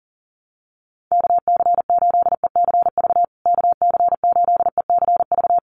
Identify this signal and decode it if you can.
Morse code, heard as a keyed tone, about 37 wpm, 700 Hz. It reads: KC8ECV KC8ECV
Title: USA Call Signs Repeated 2x 40wpm